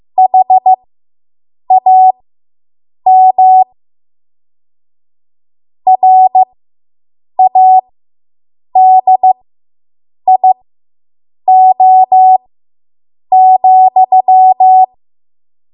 Morse code, heard as a keyed tone, about 15 wpm, spaced as for 7 wpm, 750 Hz. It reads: HAM RADIO,